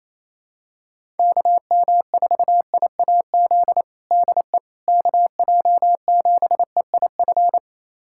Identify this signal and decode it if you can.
Morse code, heard as a keyed tone, about 28 wpm, 700 Hz. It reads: KM4IAZ DE KJ7EIF
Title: USA Callsign DE Callsign 28wpm